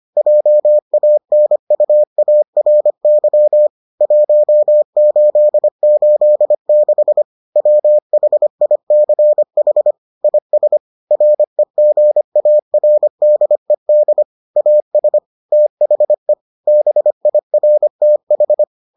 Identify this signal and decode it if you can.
Morse code, heard as a keyed tone, about 25 wpm, 600 Hz. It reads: JANUARY 1886 WHICH IS REGARDED AS THE BIRTH